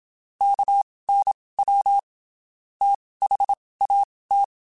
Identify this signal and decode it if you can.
Morse code, heard as a keyed tone, about 27 wpm, spaced as for 19 wpm, 800 Hz. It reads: KNW THAT
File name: (spaced) / QSOs Spaced-mp3s / WD0FBT @25 sp